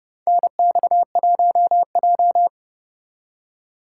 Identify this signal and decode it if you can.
Morse code, heard as a keyed tone, about 30 wpm, 700 Hz. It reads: NX1J
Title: USA Call Signs - 4 Characters - 2x Character Spacing - 30wpm